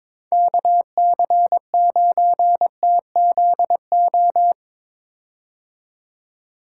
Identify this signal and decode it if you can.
Morse code, heard as a keyed tone, about 22 wpm, 700 Hz. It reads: KC9TZO
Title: K1USN SST QSO Practice - 22wpm